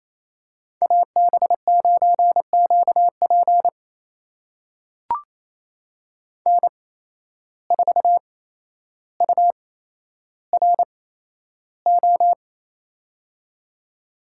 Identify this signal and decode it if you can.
Morse code, heard as a keyed tone, about 28 wpm, spaced as for 12 wpm, 700 Hz. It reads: AB9QP N4URO
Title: USA Call Signs - 5 Characters - 8x Character Spacing - 28wpm